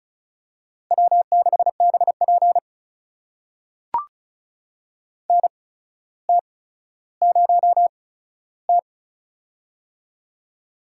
Morse code, heard as a keyed tone, about 36 wpm, spaced as for 15 wpm, 700 Hz. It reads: W6BP NT0T
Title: USA Call Signs - 4 Characters - 8x Character Spacing - 35wpm